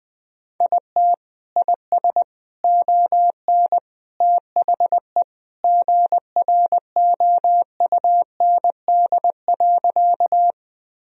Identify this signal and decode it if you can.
Morse code, heard as a keyed tone, about 21 wpm, 700 Hz. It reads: IT IS ON THE GROUND.
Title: Sentences from Top 1000 Words - Review Part 2 - 20wpm